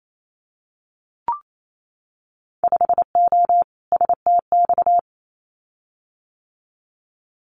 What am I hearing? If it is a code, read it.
Morse code, heard as a keyed tone, about 28 wpm, 700 Hz. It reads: 5O STX
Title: Winter Field Day Exchanges 28wpm